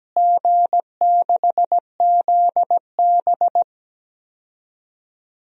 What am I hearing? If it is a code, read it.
Morse code, heard as a keyed tone, about 17 wpm, 700 Hz. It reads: G6ZB
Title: DXCC-100 Call Signs - 4 Characters - Repeated 3x - 17wpm